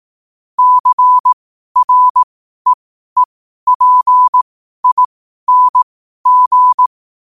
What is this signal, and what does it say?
Morse code, heard as a keyed tone, about 18 wpm, spaced as for 12 wpm, 1000 Hz. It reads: CREEPING
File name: qotdfiles / 12wpm